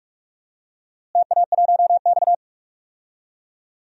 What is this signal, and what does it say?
Morse code, heard as a keyed tone, about 46 wpm, 700 Hz. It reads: TA1X
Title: DXCC-100 Call Signs - 4 Characters - Repeated 2x - 45wpm